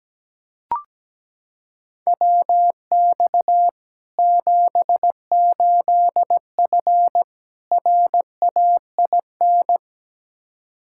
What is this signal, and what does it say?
Morse code, heard as a keyed tone, about 17 wpm, 700 Hz. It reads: WX 78F RAIN
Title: K1USN SST QSO Practice 17wpm